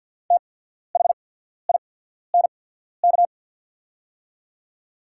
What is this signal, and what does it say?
Morse code, heard as a keyed tone, about 50 wpm, spaced as for 14 wpm, 690 Hz. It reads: THINK
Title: 8x Character Spacing - Top 200 Words - 50wpm